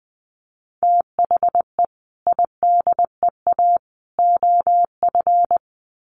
Morse code, heard as a keyed tone, about 20 wpm, 700 Hz. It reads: THE IDEA OF
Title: Sets of 3 Words 20wpm